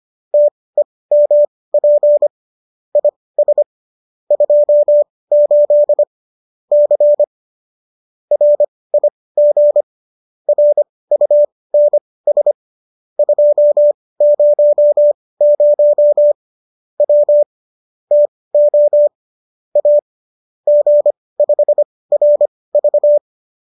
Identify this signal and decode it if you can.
Morse code, heard as a keyed tone, about 25 wpm, spaced as for 18 wpm, 590 Hz. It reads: TEMP IS 28 C RIG RUNS 200 W TO A G5RV
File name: qso204 18